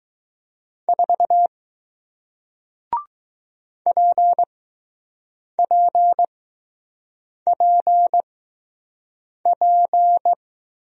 Morse code, heard as a keyed tone, about 20 wpm, spaced as for 6 wpm, 700 Hz. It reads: 4 PPPP